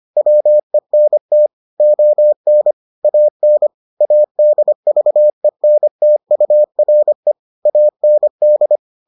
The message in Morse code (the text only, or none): WENT ON AN ADVENTURE AND